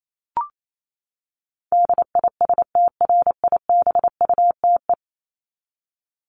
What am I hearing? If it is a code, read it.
Morse code, heard as a keyed tone, about 28 wpm, 700 Hz. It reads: DISTRIBUTE
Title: Top 1501-1750 Words 28wpm